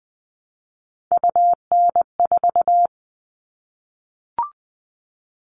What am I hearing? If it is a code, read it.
Morse code, heard as a keyed tone, about 20 wpm, 700 Hz. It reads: UN4
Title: All Cut Numbers - 3-Digit 20wpm